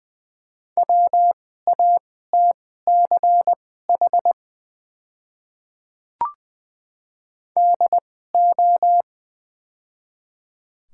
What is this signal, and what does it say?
Morse code, heard as a keyed tone, about 20 wpm, spaced as for 14 wpm, 690 Hz. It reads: WATCH DO
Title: 2x Character Spacing - Top 500 Words - 20wpm